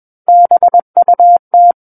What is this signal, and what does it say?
Morse code, heard as a keyed tone, about 21 wpm, 700 Hz. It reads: BUT
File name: Book One / 12-Chapter 12 - 21 WPM